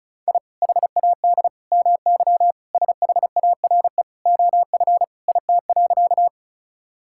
Morse code, heard as a keyed tone, about 36 wpm, 700 Hz. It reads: I HAD MY SHARE OF IT.
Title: Sentences from Top 1000 Words - Review Part 6 - 35wpm